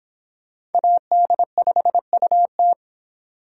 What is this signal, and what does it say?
Morse code, heard as a keyed tone, about 30 wpm, 700 Hz. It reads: AD5UT